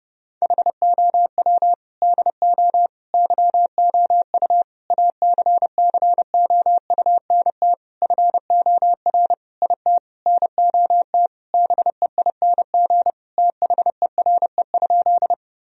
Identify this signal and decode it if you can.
Morse code, heard as a keyed tone, about 30 wpm, 700 Hz. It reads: HOW DO YOU ACCOUNT FOR IT NOT BEING THERE?
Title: Sentences from Top 1000 Words - Review Part 3 - 30wpm